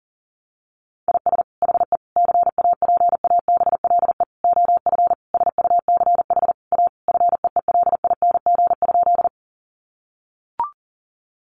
Morse code, heard as a keyed tone, about 40 wpm, 700 Hz. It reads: IS HE CAPABLE OF SUCH A FEELING?
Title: Sentences from Top 1200 Words 40wpm